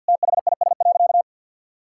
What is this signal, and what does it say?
Morse code, heard as a keyed tone, about 52 wpm, 700 Hz. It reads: THIS.
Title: Sentences from Top 1000 Words - Review Part 6 - 50wpm